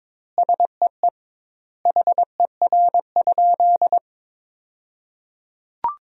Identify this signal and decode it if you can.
Morse code, heard as a keyed tone, about 22 wpm, 700 Hz. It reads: SEE HER?
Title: 2x Word-Spacing - Sentences from Top 100 Words - 22wpm